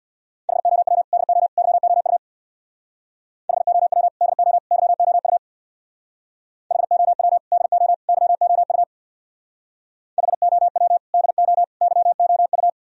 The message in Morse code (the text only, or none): HOW DO YOU HOW DO YOU HOW DO YOU HOW DO YOU